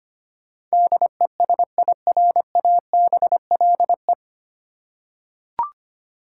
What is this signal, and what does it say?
Morse code, heard as a keyed tone, about 25 wpm, 700 Hz. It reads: DESIRABLE
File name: Nine Letter Words - 25wpm